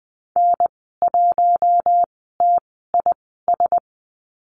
Morse code, heard as a keyed tone, about 20 wpm, spaced as for 14 wpm, 700 Hz. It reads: N1TIS
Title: 048 Lesson 9g - Number 4 - Call Signs 20wpm